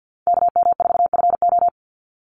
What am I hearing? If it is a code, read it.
Morse code, heard as a keyed tone, about 48 wpm, 700 Hz. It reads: KM4FO
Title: Call Signs Repeated 3x - 50wpm